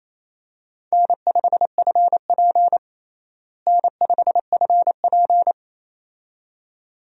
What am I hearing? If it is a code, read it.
Morse code, heard as a keyed tone, about 28 wpm, 700 Hz. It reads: N5FP N5FP
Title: USA Call Signs - 4 Characters - Repeated 2x - 28wpm